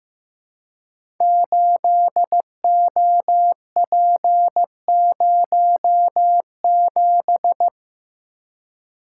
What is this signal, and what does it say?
Morse code, heard as a keyed tone, about 15 wpm, 700 Hz. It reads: 8OP07
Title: Code Groups - 5 Letter:Numbers - 15wpm